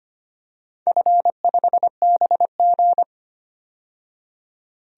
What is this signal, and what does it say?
Morse code, heard as a keyed tone, about 25 wpm, 700 Hz. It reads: F5BG
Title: DXCC-100 Call Signs - 4 Characters - 25wpm